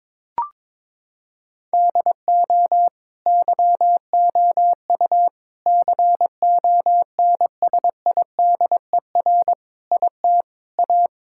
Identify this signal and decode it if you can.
Morse code, heard as a keyed tone, about 22 wpm, 700 Hz. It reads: DO YOU CONSIDER IT A
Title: Sentences from Top 1000 Words - Review Part 6 - 22wpm